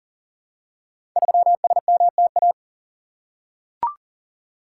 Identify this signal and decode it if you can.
Morse code, heard as a keyed tone, about 40 wpm, 700 Hz. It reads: 3SMTA
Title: Code Groups - 5 Letter:Numbers - 40wpm